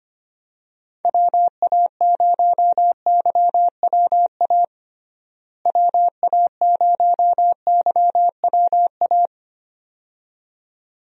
Morse code, heard as a keyed tone, about 25 wpm, 700 Hz. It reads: WA0YWA WA0YWA
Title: USA Call Signs - 6 Characters - Repeated 2x - 25wpm